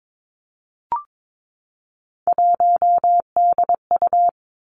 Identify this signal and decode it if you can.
Morse code, heard as a keyed tone, about 22 wpm, 700 Hz. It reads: 1DU
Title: Three Challenging Letter-Number Combinations 22wpm